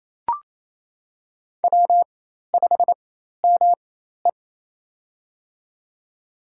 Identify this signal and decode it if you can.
Morse code, heard as a keyed tone, about 28 wpm, spaced as for 13 wpm, 700 Hz. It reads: W5ME
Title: USA Call Signs - 4 Characters - 4x Character Spacing - 28wpm